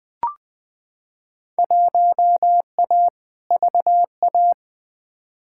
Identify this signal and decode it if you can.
Morse code, heard as a keyed tone, about 20 wpm, 700 Hz. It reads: 1A VA
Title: Field Day Exchanges - 20wpm